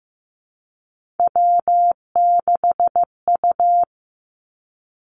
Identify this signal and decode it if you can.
Morse code, heard as a keyed tone, about 15 wpm, 700 Hz. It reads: W6U
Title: Three Challenging Letter-Number Combinations 15wpm